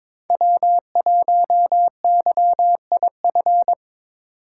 Morse code, heard as a keyed tone, about 21 wpm, 700 Hz. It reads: W1YIF
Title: USA Call Signs - 5 Characters - 4x Character Spacing - 22wpm